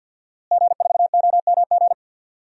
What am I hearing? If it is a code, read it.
Morse code, heard as a keyed tone, about 52 wpm, 700 Hz. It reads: G4OMG
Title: DXCC-100 Call Signs - 5 Characters - Repeated 3x - 50wpm